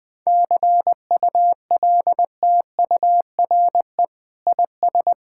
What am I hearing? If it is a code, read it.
Morse code, heard as a keyed tone, about 20 wpm, 700 Hz. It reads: CULTURE IS
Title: Sentences from Top 1000 Words - Review Part 6 - 20wpm